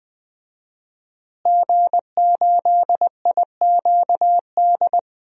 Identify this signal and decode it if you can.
Morse code, heard as a keyed tone, about 20 wpm, 700 Hz. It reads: G8IQD
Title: DXCC-100 Call Signs - 5 Characters - Repeated 2x - 20wpm